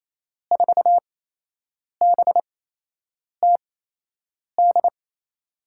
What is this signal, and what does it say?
Morse code, heard as a keyed tone, about 28 wpm, spaced as for 8 wpm, 700 Hz. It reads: 4BTD
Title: USA Call Signs - 6 Characters - 8x Character Spacing - 28wpm